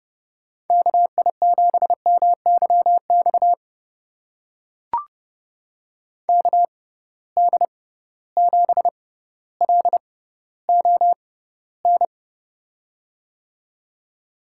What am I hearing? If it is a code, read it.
Morse code, heard as a keyed tone, about 30 wpm, spaced as for 15 wpm, 700 Hz. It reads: KI7MYX KD7LON
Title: USA Call Signs - 6 Characters - 6x Character Spacing - 30wpm